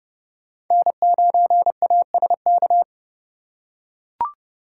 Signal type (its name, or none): Morse code